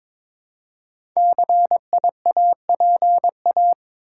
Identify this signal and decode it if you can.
Morse code, heard as a keyed tone, about 22 wpm, 700 Hz. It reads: CIAPA
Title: Code Groups - 5 Letters - 22wpm